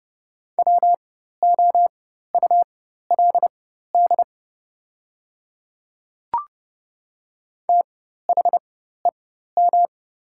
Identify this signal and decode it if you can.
Morse code, heard as a keyed tone, about 30 wpm, spaced as for 14 wpm, 690 Hz. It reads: WOULD THEM
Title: 4x Character Spacing - Top 100 Words - 30wpm